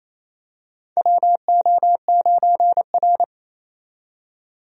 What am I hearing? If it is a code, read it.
Morse code, heard as a keyed tone, about 28 wpm, 700 Hz. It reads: WO9R
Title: USA Call Signs - 4 Characters - 28wpm